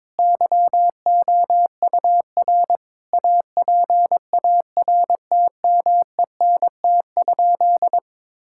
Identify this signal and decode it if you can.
Morse code, heard as a keyed tone, about 21 wpm, 700 Hz. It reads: YOUR APARTMENT?
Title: Sentences from Top 1000 Words - Review Part 6 - 22wpm